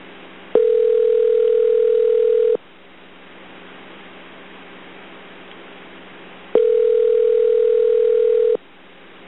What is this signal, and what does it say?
Ringback tone (North America), 440 + 480 Hz.